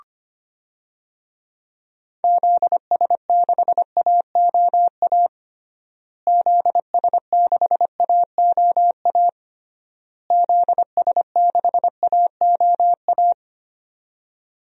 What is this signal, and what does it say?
Morse code, heard as a keyed tone, about 25 wpm, 700 Hz. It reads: ZS6AOA ZS6AOA ZS6AOA
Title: DXCC-100 Call Signs - 6 Characters - Repeated 3x - 25wpm